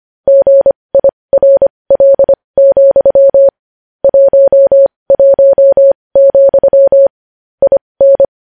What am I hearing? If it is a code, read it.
Morse code, heard as a keyed tone, about 25 wpm, spaced as for 20 wpm, 550 Hz. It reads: GIRL, 11, IN
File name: podcast / News headlines-1x-25x20